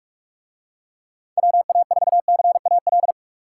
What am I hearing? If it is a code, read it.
Morse code, heard as a keyed tone, about 41 wpm, 700 Hz. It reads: WA4CAL